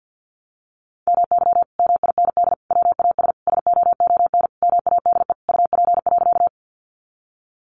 Morse code, heard as a keyed tone, about 50 wpm, 700 Hz. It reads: MY MIND WAS SOON MADE UP.
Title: Sentences from Top 500 Words 50wpm